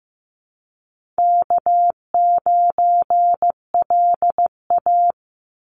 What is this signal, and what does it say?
Morse code, heard as a keyed tone, about 15 wpm, 700 Hz. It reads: K9LA
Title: Easy Call Signs Repeated 2x - 15wpm